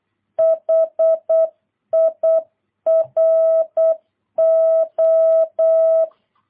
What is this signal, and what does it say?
Morse code, heard as a keyed tone, about 8 wpm, 650 Hz. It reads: HIRO
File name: jh8cbh / 2022cw